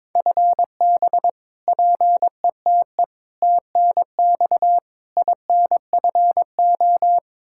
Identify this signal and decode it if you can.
Morse code, heard as a keyed tone, about 22 wpm, 700 Hz. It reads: FB PETE TNX INFO
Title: Practice QSO - Exchange 3 - More Info - 22wpm